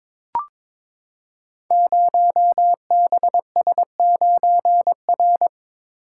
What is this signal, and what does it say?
Morse code, heard as a keyed tone, about 22 wpm, 700 Hz. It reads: 0BS9R
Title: Code Groups - 5 Letter:Numbers - 22wpm